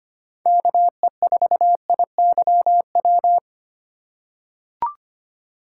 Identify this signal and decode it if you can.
Morse code, heard as a keyed tone, about 25 wpm, 700 Hz. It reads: KE4IYW
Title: USA Call Signs - 6 Characters - 3x Character Spacing - 25wpm